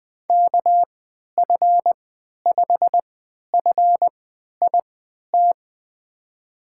Morse code, heard as a keyed tone, about 21 wpm, spaced as for 11 wpm, 700 Hz. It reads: KF5FIT
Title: USA Call Signs - 6 Characters - 3x Character Spacing - 20wpm